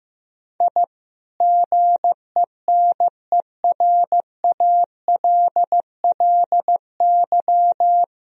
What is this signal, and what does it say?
Morse code, heard as a keyed tone, about 15 wpm, 700 Hz. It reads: I GENERALLY
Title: Sentences from Top 1000 Words - Review Part 4 - 15wpm